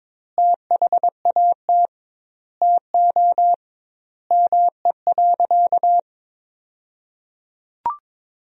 Morse code, heard as a keyed tone, about 22 wpm, 700 Hz. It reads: THAT TO ME.
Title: 2x Word-Spacing - Sentences from Top 400 Words - 22wpm